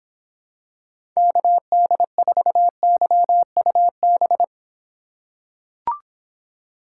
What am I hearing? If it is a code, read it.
Morse code, heard as a keyed tone, about 26 wpm, 700 Hz. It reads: KD4YUB